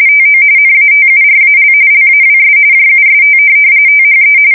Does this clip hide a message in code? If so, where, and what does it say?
Yes: RTTY from 0.1 to 4.4 s, about 45.45 baud. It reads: TO NATURAL LAWS JUST THE